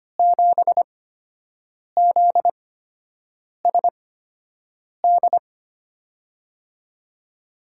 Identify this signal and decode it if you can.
Morse code, heard as a keyed tone, about 25 wpm, spaced as for 7 wpm, 700 Hz. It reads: 7ZSD
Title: USA Call Signs - 5 Characters - 8x Character Spacing - 25wpm